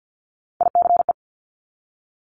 Morse code, heard as a keyed tone, about 50 wpm, 700 Hz. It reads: ICE
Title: Multi-letter Phonemes Repeated 3x 50wpm